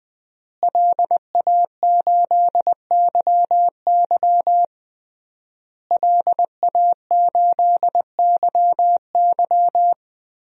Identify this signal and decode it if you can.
Morse code, heard as a keyed tone, about 20 wpm, 700 Hz. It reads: LA8YY LA8YY
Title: DXCC-100 Call Signs - 5 Characters - Repeated 3x - 20wpm